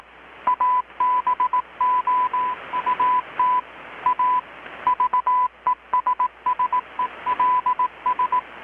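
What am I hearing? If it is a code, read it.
Morse code, heard as a keyed tone, about 18 wpm, 1000 Hz. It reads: ABOUT A VESSELS
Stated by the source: Morse from KPH's Night of Nights XX maritime radiotelegraphy event, received in Chiapas, Mexico about 3,800 km away on a Kenwood R-600 with a grounded 10 m long-wire antenna.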